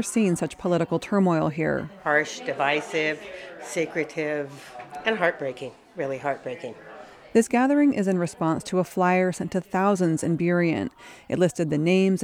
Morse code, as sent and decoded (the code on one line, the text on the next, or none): none